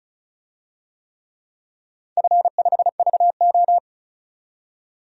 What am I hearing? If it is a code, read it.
Morse code, heard as a keyed tone, about 36 wpm, 700 Hz. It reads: F5VO
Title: DXCC-100 Call Signs - 4 Characters - 35wpm